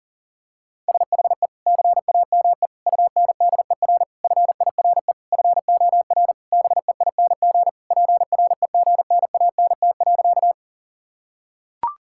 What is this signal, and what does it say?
Morse code, heard as a keyed tone, about 37 wpm, 700 Hz. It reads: SHE CAME UNDER FIRE FOR BEING PREGNANT.
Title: Sentences from Top 1000 Words - Review Part 6 - 40wpm